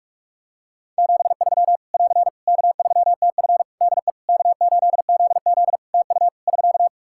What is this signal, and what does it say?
Morse code, heard as a keyed tone, about 46 wpm, 700 Hz. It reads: 73 <AR> K3TF DE K8ZZ TU <SK>
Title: K1USN SST QSO Practice 45wpm